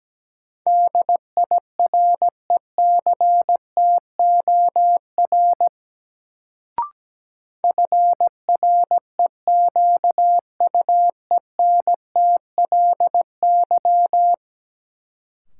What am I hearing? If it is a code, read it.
Morse code, heard as a keyed tone, about 17 wpm, 700 Hz. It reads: DIRECTOR FREQUENTLY
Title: Sentences from Top 1300 Words - New Word Drill - 17wpm